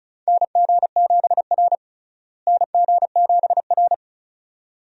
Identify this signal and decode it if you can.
Morse code, heard as a keyed tone, about 36 wpm, 700 Hz. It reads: NG7R NG7R
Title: USA Call Signs - 4 Characters - Repeated 2x - 35wpm